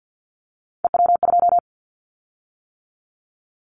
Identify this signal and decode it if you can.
Morse code, heard as a keyed tone, about 50 wpm, 700 Hz. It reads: EW2
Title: Callsign Prefixes for Top 100 DXCC - 50wpm